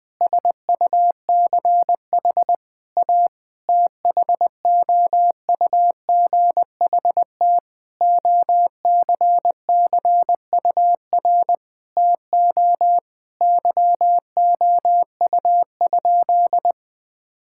Morse code, heard as a keyed tone, about 19 wpm, 700 Hz. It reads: SUCH A THOUGHT OCCUR TO YOU?
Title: Sentences from Top 1000 Words - Review Part 6 - 20wpm